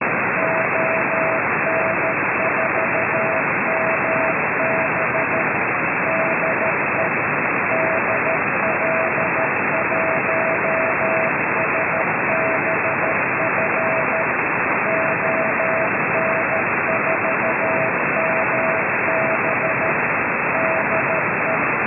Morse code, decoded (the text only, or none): ON4MB DE DL1SBF ON4MB DE